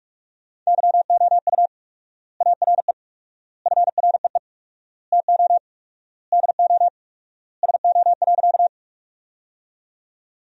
Morse code, heard as a keyed tone, about 41 wpm, 700 Hz. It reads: YOU ARE FREE TO DO SO.
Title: 4x Word-Spacing - Sentences from Top 500 Words - 45wpm